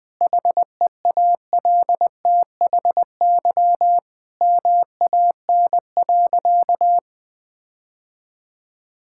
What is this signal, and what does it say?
Morse code, heard as a keyed tone, about 20 wpm, 700 Hz. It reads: HEALTHY MAN.
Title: Sentences from Top 1000 Words - Review Part 5 - 20wpm